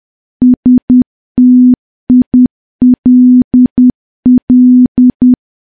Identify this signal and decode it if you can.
Morse code, heard as a keyed tone, about 10 wpm, 250 Hz. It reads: STILL
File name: morse / morse-training-10WPM